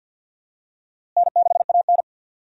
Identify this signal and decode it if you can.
Morse code, heard as a keyed tone, about 52 wpm, 700 Hz. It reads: N6AN
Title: USA Call Signs - 4 Characters - Repeated 2x - 50wpm